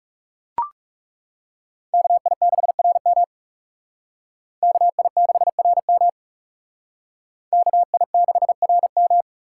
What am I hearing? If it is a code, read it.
Morse code, heard as a keyed tone, about 37 wpm, 700 Hz. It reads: KI6RM KI6RM KI6RM
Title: Speed Racing - USA 2x2 Call Signs 30wpm